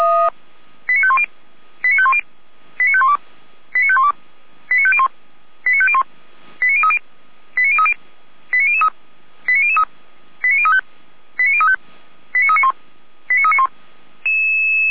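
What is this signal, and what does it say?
Five-tone selective call (ZVEI), heard as quick runs of five tones, tones of about 70 ms. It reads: 89510 89510 89512 89512 89501 89501 89030 89030 89003 89003 89035 89035 89391 89391
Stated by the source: Five-tone ZVEI selective call received by SDR.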